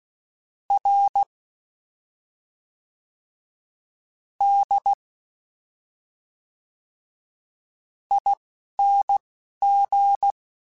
Morse code, heard as a keyed tone, about 16 wpm, 800 Hz. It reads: R D I N G
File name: Special INGs / Special INGs-mp3s / Fast ing @10